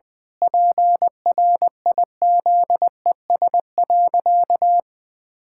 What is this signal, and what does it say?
Morse code, heard as a keyed tone, about 20 wpm, 700 Hz. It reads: PRIZES.
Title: Pangram Challenge v2 - 20wpm